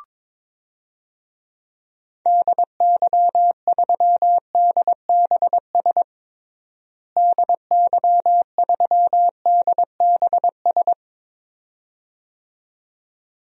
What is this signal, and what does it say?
Morse code, heard as a keyed tone, about 22 wpm, 700 Hz. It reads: DY3DBS DY3DBS
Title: DXCC-100 Call Signs - 6 Characters - Repeated 2x - 22wpm